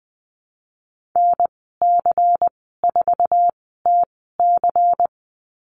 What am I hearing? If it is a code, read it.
Morse code, heard as a keyed tone, about 20 wpm, spaced as for 14 wpm, 700 Hz. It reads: NC4TC
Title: 167 Lesson 27g - Number 7 - Call Signs 20wpm